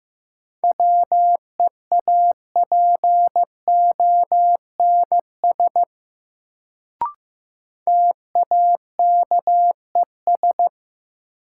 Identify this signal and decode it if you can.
Morse code, heard as a keyed tone, about 15 wpm, 700 Hz. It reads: WEAPONS TAKES
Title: Sets of 5 Words Encore - New Word Drill - 15wpm